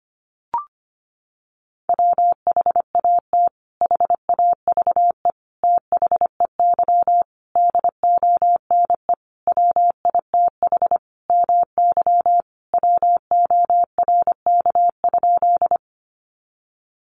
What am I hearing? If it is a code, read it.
Morse code, heard as a keyed tone, about 25 wpm, 700 Hz. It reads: WHAT HAVE THEY DONE WITH MY WORK?
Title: Sentences from Top 500 Words 25wpm